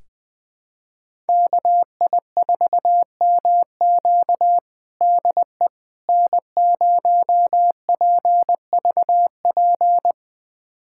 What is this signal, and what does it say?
Morse code, heard as a keyed tone, about 20 wpm, 700 Hz. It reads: KI4MQ DE N0PVP
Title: USA Callsign DE Callsign - 5 Characters - 20wpm